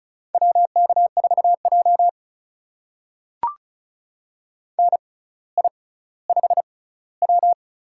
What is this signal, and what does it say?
Morse code, heard as a keyed tone, about 34 wpm, spaced as for 18 wpm, 700 Hz. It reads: WK4J NI5W
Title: USA Call Signs - 6x Character Spacing - 35wpm